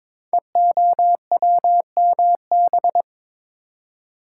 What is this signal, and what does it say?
Morse code, heard as a keyed tone, about 22 wpm, 700 Hz. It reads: EOWMB
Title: Code Groups - 5 Letter:Numbers - 22wpm